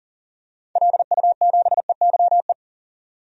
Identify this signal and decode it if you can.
Morse code, heard as a keyed tone, about 39 wpm, 700 Hz. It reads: LU7EYE
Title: DXCC-100 Call Signs - 6 Characters - 40wpm